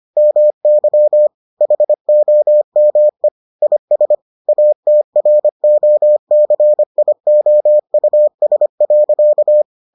Morse code, heard as a keyed tone, about 25 wpm, 600 Hz. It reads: MY HOME IS ATROCIOUS.